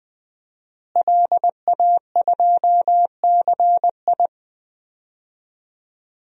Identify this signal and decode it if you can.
Morse code, heard as a keyed tone, about 20 wpm, 700 Hz. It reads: LA2CI
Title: DXCC-100 Call Signs - 5 Characters - Repeated 2x - 20wpm